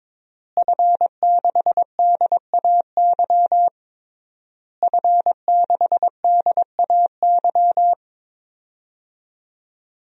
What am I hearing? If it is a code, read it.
Morse code, heard as a keyed tone, about 22 wpm, 700 Hz. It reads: F6DAY F6DAY
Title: DXCC-100 Call Signs - 5 Characters - Repeated 3x - 22wpm